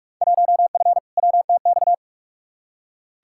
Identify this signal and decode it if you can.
Morse code, heard as a keyed tone, about 41 wpm, 700 Hz. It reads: 1F WTX
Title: Field Day Exchanges Encore - 45wpm